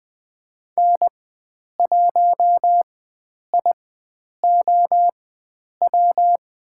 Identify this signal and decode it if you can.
Morse code, heard as a keyed tone, about 20 wpm, spaced as for 9 wpm, 700 Hz. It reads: N1IOW
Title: USA Call Signs - 5 Characters - 4x Character Spacing - 20wpm